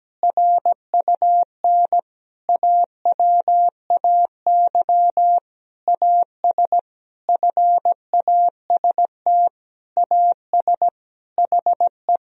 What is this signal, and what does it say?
Morse code, heard as a keyed tone, about 17 wpm, 700 Hz. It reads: RUN AWAY AS FAST AS HE
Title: Sentences from Top 1000 Words - Review Part 2 - 17wpm